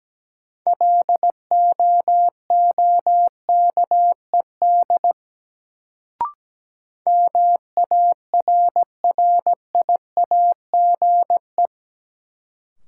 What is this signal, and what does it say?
Morse code, heard as a keyed tone, about 17 wpm, 700 Hz. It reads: LOOKED MARRIAGE
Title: Sentences from Top 800 Words - New Word Drill - 17wpm